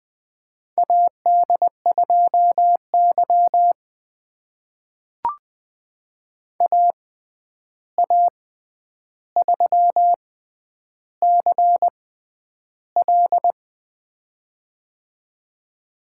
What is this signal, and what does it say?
Morse code, heard as a keyed tone, about 20 wpm, spaced as for 10 wpm, 700 Hz. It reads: AD2Y AA3CL
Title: USA Call Signs - 6x Character Spacing - 20wpm